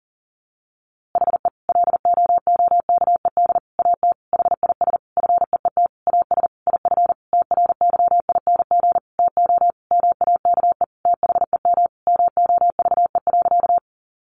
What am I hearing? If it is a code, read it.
Morse code, heard as a keyed tone, about 40 wpm, 700 Hz. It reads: HE LOOKED AT HIS FEET AS IF TRYING TO MAKE THEM MOVE.
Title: A Sound Of Thunder 40wpm